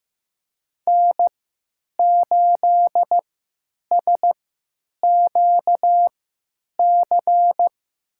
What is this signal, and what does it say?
Morse code, heard as a keyed tone, about 15 wpm, spaced as for 9 wpm, 700 Hz. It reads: N8SQC
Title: USA Call Signs - 5 Characters - 3x Character Spacing - 15wpm